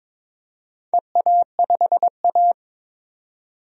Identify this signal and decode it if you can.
Morse code, heard as a keyed tone, about 22 wpm, 700 Hz. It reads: EA5A
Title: DXCC-100 Call Signs - 4 Characters - 22wpm